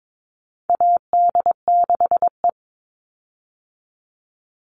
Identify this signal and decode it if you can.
Morse code, heard as a keyed tone, about 22 wpm, 700 Hz. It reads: AD6E
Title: Call Signs - 22wpm